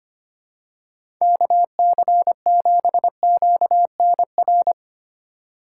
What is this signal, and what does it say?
Morse code, heard as a keyed tone, about 25 wpm, 700 Hz. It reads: KC7QNR
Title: USA Call Signs - 2x Character Spacing - 25wpm